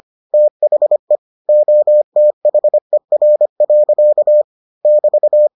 Morse code, heard as a keyed tone, about 25 wpm, 600 Hz. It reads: THE OTHER. <BT>